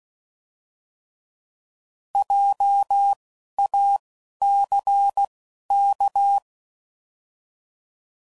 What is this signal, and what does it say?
Morse code, heard as a keyed tone, about 16 wpm, spaced as for 12 wpm, 800 Hz. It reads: JACK